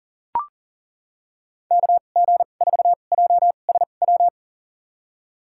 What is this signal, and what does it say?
Morse code, heard as a keyed tone, about 40 wpm, spaced as for 29 wpm, 700 Hz. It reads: KG4JSW
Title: USA Call Signs - 6 Characters - 2x Character Spacing - 40wpm